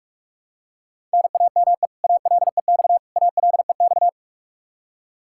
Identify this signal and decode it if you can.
Morse code, heard as a keyed tone, about 46 wpm, 700 Hz. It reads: NAME ALEX ALEX
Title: Practice QSO - Exchange 1 - Beginning - 45wpm